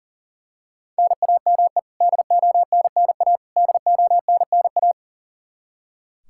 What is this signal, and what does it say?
Morse code, heard as a keyed tone, about 40 wpm, 700 Hz. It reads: NAME DONNA DONNA
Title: Practice QSO - Exchange 1 - Beginning - 40wpm